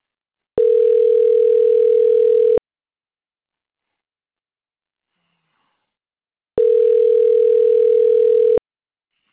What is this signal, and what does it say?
Ringback tone (North America), 440 + 480 Hz.